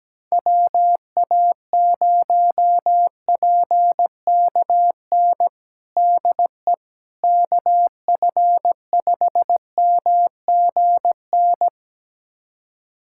Morse code, heard as a keyed tone, about 17 wpm, 700 Hz. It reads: WA0PKN DE KF5MGN
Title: USA Callsign DE Callsign - 6 Characters - 17wpm